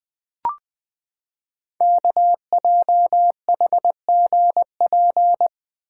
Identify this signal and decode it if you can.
Morse code, heard as a keyed tone, about 20 wpm, 700 Hz. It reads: KJHGP
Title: Code Groups - 5 Letters - 20wpm